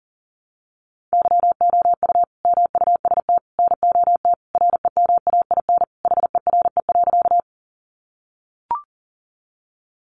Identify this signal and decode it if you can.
Morse code, heard as a keyed tone, about 40 wpm, 700 Hz. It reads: YOU MUST NOT REMAIN HERE.
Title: Sentences from Top 1200 Words 40wpm